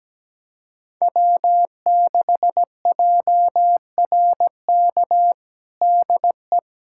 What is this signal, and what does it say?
Morse code, heard as a keyed tone, about 17 wpm, 700 Hz. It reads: W6JRK DE
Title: USA Callsign DE Callsign - 5 Characters - 17wpm